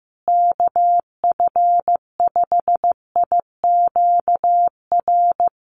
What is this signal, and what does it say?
Morse code, heard as a keyed tone, about 15 wpm, 700 Hz. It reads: KF5IQR
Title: USA Call Signs Repeated 3x 15wpm